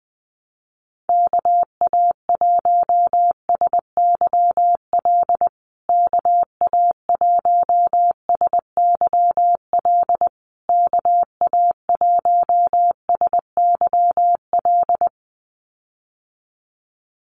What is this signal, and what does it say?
Morse code, heard as a keyed tone, about 20 wpm, 700 Hz. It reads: KA1SYL KA1SYL KA1SYL
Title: USA Call Signs Repeated 3x 20wpm